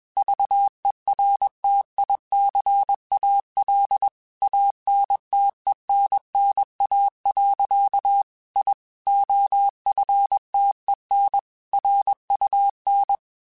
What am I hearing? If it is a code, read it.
Morse code, heard as a keyed tone, about 21 wpm, 800 Hz. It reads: VERTICAL ANTENNA. I OFTEN RUN
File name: QSOs / QSOs-mp3s / kc7sac@20